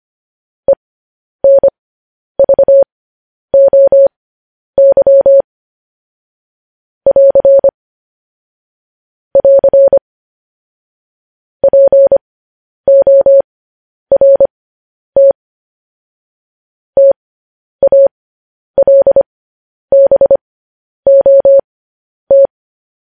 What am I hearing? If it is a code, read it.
Morse code, heard as a keyed tone, about 25 wpm, spaced as for 10 wpm, 550 Hz. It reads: ENVOY <AR> <AR> PORT TALBOT
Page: News Headlines in Morse Code at 10 WPM